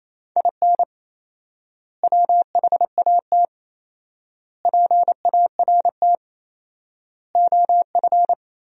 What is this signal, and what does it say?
Morse code, heard as a keyed tone, about 27 wpm, 700 Hz. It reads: IN WHAT PART OF
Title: 4x Word-Spacing - Sentences from Top 200 Words - 28wpm